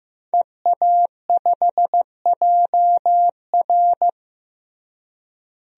Morse code, heard as a keyed tone, about 16 wpm, 700 Hz. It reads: EA5JR
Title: DXCC-100 Call Signs - 5 Characters - Repeated 2x - 15wpm